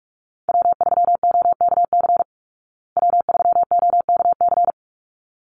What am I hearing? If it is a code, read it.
Morse code, heard as a keyed tone, about 44 wpm, 700 Hz. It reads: W3OKC W3OKC
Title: Call Signs Repeated 3x - 45wpm